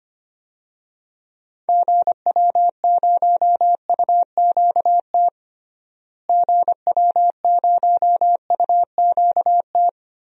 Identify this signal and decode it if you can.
Morse code, heard as a keyed tone, about 25 wpm, 700 Hz. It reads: GW0UQT GW0UQT
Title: DXCC-100 Call Signs - 6 Characters - Repeated 2x - 25wpm